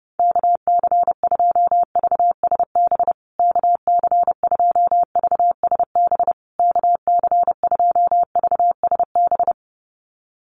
Morse code, heard as a keyed tone, about 30 wpm, 700 Hz. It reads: KC2VSB KC2VSB KC2VSB
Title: USA Call Signs Repeated 3x 30wpm